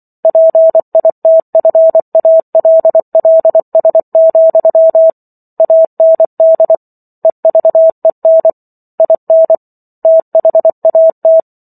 Morse code, heard as a keyed tone, about 24 wpm, 650 Hz. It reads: PITFALLS, AND EVEN IN THAT
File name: Chapter 15 - 24 WPM